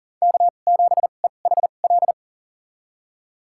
Morse code, heard as a keyed tone, about 38 wpm, spaced as for 28 wpm, 700 Hz. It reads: K7EHL
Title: USA Call Signs - 5 Characters - 2x Character Spacing - 40wpm